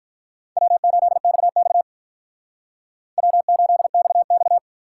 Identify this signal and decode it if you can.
Morse code, heard as a keyed tone, about 50 wpm, 700 Hz. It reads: W8XX W8XX